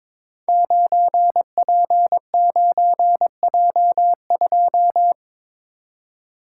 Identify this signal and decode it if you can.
Morse code, heard as a keyed tone, about 22 wpm, 700 Hz. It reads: 9P9J2
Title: Code Groups - 5 Letter:Numbers - 22wpm